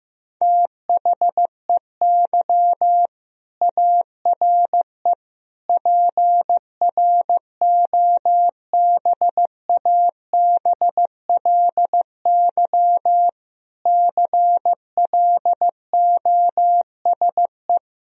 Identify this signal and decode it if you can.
Morse code, heard as a keyed tone, about 15 wpm, 700 Hz. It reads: THEY ARE PROBABLY CLOSE
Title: Sentences from Top 1000 Words - Review Part 4 - 15wpm